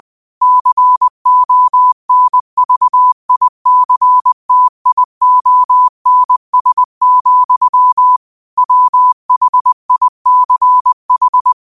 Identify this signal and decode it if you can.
Morse code, heard as a keyed tone, about 20 wpm, 1000 Hz. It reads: CONVICTIONS, WHICH
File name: qotdfiles / 20wpm